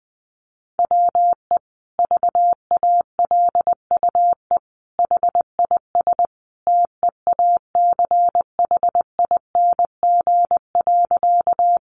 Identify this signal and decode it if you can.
Morse code, heard as a keyed tone, about 20 wpm, 700 Hz. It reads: WE VALUE HIS TEACHING.
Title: Sentences from Top 1100 Words 20wpm